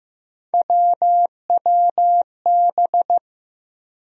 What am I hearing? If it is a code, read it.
Morse code, heard as a keyed tone, about 15 wpm, 700 Hz. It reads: WWB
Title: USA Call Sign Postfixes - 3 Letters 15wpm